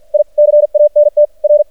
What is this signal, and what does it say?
Morse code, heard as a keyed tone, about 50 wpm, spaced as for 44 wpm, 590 Hz. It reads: I CANT R
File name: archived / CRSnet